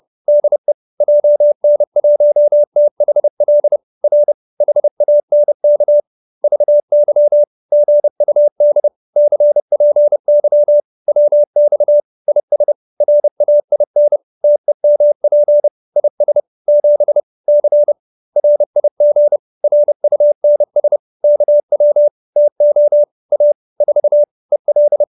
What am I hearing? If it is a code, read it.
Morse code, heard as a keyed tone, about 30 wpm, 590 Hz. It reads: DE JN1THL R HANK VY GUD CPY WX IS RAIN TEMP IS 7 C RIG RUNS KW TO A 4 EL